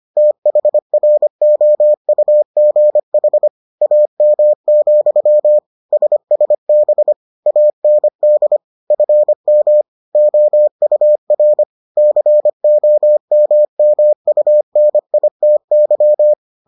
Morse code, heard as a keyed tone, about 25 wpm, 600 Hz. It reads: THROUGH AM, SSB AND FM OUR COMMUNITY